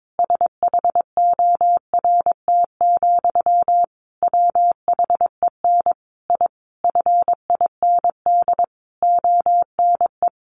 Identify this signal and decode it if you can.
Morse code, heard as a keyed tone, about 22 wpm, 700 Hz. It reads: SHORT, WHEN I FIND ONE
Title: A Sound Of Thunder 22wpm